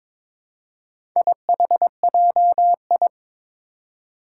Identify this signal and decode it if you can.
Morse code, heard as a keyed tone, about 21 wpm, 700 Hz. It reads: IHJI
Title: Four Challenging Letter-Number Combinations 22wpm